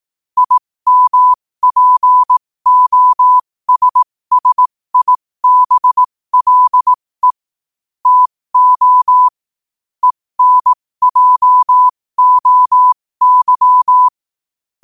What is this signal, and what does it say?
Morse code, heard as a keyed tone, about 18 wpm, spaced as for 15 wpm, 1000 Hz. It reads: IMPOSSIBLE TO ENJOY
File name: qotdfiles / 15wpm